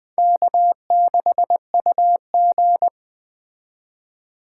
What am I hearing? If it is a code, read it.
Morse code, heard as a keyed tone, about 20 wpm, 700 Hz. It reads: K6UG